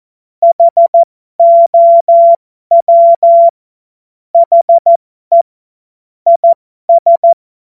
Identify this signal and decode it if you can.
Morse code, heard as a keyed tone, about 14 wpm, spaced as for 12 wpm, 680 Hz. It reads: HOW HE IS
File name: Morsekurs / Witze